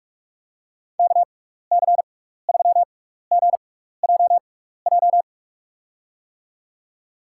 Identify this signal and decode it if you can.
Morse code, heard as a keyed tone, about 46 wpm, spaced as for 16 wpm, 700 Hz. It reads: KC3GJJ
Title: USA Call Signs - 6 Characters - 6x Character Spacing - 45wpm